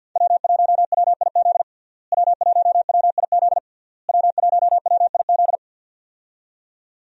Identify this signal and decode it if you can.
Morse code, heard as a keyed tone, about 46 wpm, 700 Hz. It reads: W1WIZ W1WIZ W1WIZ
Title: USA Call Signs - 5 Characters - Repeated 3x - 50wpm